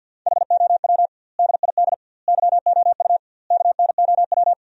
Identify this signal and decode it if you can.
Morse code, heard as a keyed tone, about 52 wpm, 700 Hz. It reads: HOW DID YOU KNOW
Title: 2x Word-Spacing - Sentences from Top 100 Words - 50wpm